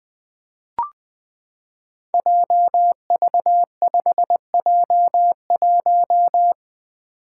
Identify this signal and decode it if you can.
Morse code, heard as a keyed tone, about 20 wpm, 700 Hz. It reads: JV5J1
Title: Five Challenging Letter-Number Combinations 20wpm